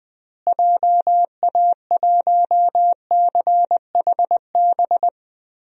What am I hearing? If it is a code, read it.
Morse code, heard as a keyed tone, about 20 wpm, 700 Hz. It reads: JA1CHB
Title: DXCC-100 Call Signs - 6 Characters - Repeated 3x - 20wpm